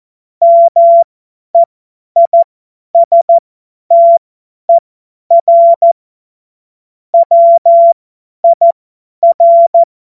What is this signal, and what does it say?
Morse code, heard as a keyed tone, about 14 wpm, spaced as for 10 wpm, 680 Hz. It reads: MEISTER WIR
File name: Morsekurs / Witze / Witz 17.Oktober 2023 50 F70